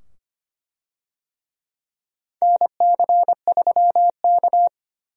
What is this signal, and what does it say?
Morse code, heard as a keyed tone, about 25 wpm, 700 Hz. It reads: NC3K
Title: USA Call Signs - 4 Characters - 25wpm